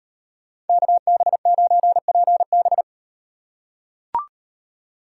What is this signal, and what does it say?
Morse code, heard as a keyed tone, about 38 wpm, 700 Hz. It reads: KB9PB